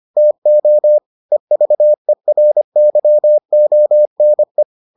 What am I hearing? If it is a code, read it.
Morse code, heard as a keyed tone, about 25 wpm, 600 Hz. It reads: TO EVERYONE